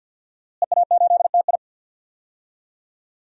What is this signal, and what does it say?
Morse code, heard as a keyed tone, about 52 wpm, 700 Hz. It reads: EA8TI
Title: DXCC-100 Call Signs - 5 Characters - Repeated 2x - 50wpm